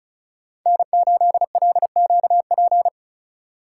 Morse code, heard as a keyed tone, about 34 wpm, 700 Hz. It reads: N8LQP
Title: USA Call Signs - 5 Characters - 4x Character Spacing - 35wpm